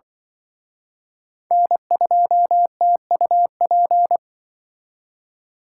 Morse code, heard as a keyed tone, about 23 wpm, 700 Hz. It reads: N2TUP
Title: Speed Racing - USA 1x3 Call Signs 20wpm